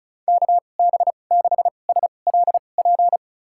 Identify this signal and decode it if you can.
Morse code, heard as a keyed tone, about 36 wpm, spaced as for 25 wpm, 700 Hz. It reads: KB6SLP
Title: USA Call Signs - 6 Characters - 2x Character Spacing - 35wpm